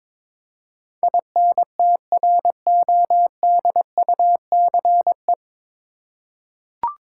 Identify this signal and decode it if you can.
Morse code, heard as a keyed tone, about 22 wpm, 700 Hz. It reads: INTRODUCE
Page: Morse Code Ninja practice 22wpm